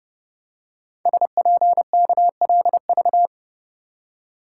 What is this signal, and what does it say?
Morse code, heard as a keyed tone, about 28 wpm, 700 Hz. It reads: SPKLV
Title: Code Groups - 5 Letters - 30wpm